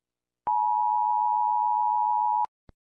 Emergency Alert System (SAME) tones (the 853 and 960 Hz attention tone). It is heard from 0.4 to 2.5 s.